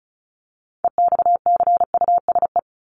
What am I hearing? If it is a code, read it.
Morse code, heard as a keyed tone, about 34 wpm, 700 Hz. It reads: EXCUSE
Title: Top 501-750 Words 35wpm